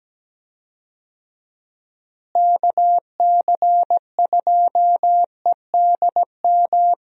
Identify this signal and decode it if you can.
Morse code, heard as a keyed tone, about 17 wpm, 700 Hz. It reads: KC2EDM
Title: USA Call Signs - 6 Characters - Repeated 2x - 17wpm